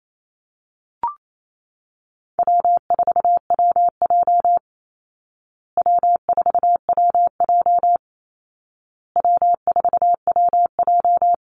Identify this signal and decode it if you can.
Morse code, heard as a keyed tone, about 28 wpm, 700 Hz. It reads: W4WJ W4WJ W4WJ
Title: Call Signs Repeated 2x - 28wpm